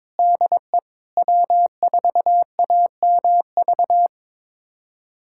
Morse code, heard as a keyed tone, about 21 wpm, 700 Hz. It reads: DE W4AMV
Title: USA Callsign DE Callsign 22wpm